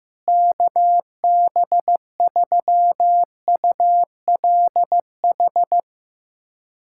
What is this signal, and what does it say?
Morse code, heard as a keyed tone, about 15 wpm, 700 Hz. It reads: KB3ULH
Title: USA Callsign DE Callsign - 6 Characters - 15wpm